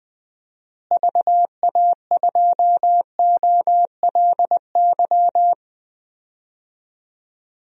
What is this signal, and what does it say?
Morse code, heard as a keyed tone, about 20 wpm, 700 Hz. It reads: VA2OLY
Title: DXCC-100 Call Signs - 6 Characters - Repeated 2x - 20wpm